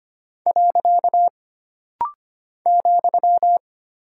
Morse code, heard as a keyed tone, about 25 wpm, 700 Hz. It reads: . ,
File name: Single Punctuation-Prosign - Rapid-Fire - 25wpm